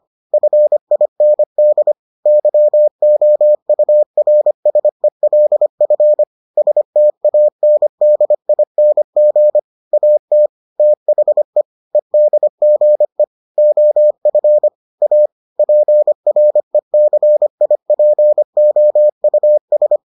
Morse code, heard as a keyed tone, about 25 wpm, 600 Hz. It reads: FIND YOURSELF STANDING AT THE EDGE OF A PRECIPOUS